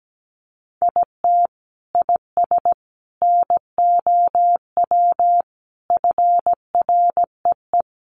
Morse code, heard as a keyed tone, about 17 wpm, 700 Hz. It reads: IT IS NOW FREE
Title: Sentences from Top 1300 Words 17wpm